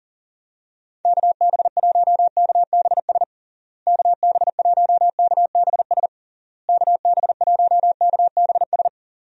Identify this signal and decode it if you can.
Morse code, heard as a keyed tone, about 40 wpm, 700 Hz. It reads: KB1KBS KB1KBS KB1KBS
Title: USA Call Signs - 6 Characters - Repeated 3x - 40wpm